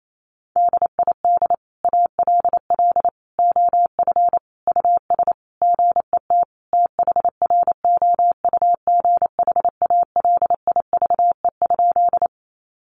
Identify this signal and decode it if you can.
Morse code, heard as a keyed tone, about 28 wpm, 700 Hz. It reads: DID ALL OF US GET THROUGH ALIVE?
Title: A Sound Of Thunder 28wpm